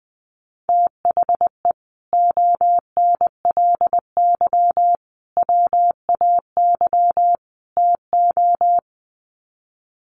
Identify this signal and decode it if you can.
Morse code, heard as a keyed tone, about 20 wpm, 700 Hz. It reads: THE ONLY WAY TO
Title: Sets of 4 Words 20wpm